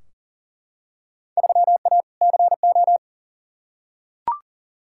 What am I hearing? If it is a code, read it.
Morse code, heard as a keyed tone, about 40 wpm, 700 Hz. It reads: 3A CO
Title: Field Day Exchanges Encore - 40wpm